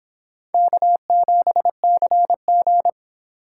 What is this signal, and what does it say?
Morse code, heard as a keyed tone, about 24 wpm, 700 Hz. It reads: K7CG